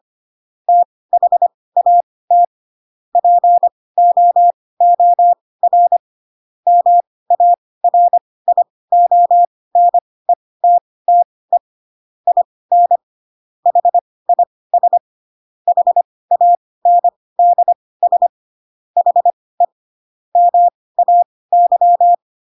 Morse code, heard as a keyed tone, about 25 wpm, spaced as for 18 wpm, 700 Hz. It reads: THAT POOR MARIONETTE IN HIS HANDS HE MAY
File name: Pinocchio ch3 12 25 ik1hge